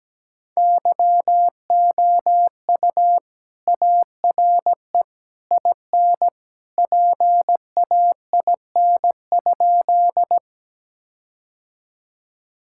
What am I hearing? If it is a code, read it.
Morse code, heard as a keyed tone, about 17 wpm, 700 Hz. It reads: YOU ARE IN PAIN?
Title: Sentences from Top 1600 Words 17wpm